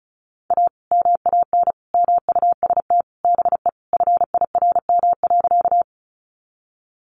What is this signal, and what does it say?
Morse code, heard as a keyed tone, about 34 wpm, 700 Hz. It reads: A MAN MUST BE FIRM.
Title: Sentences from Top 1400 Words 35wpm